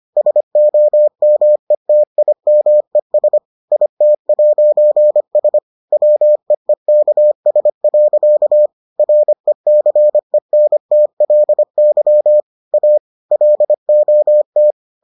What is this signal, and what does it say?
Morse code, heard as a keyed tone, about 25 wpm, 600 Hz. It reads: SOMETIMES IT'S WEEKS. RECENTLY A LOT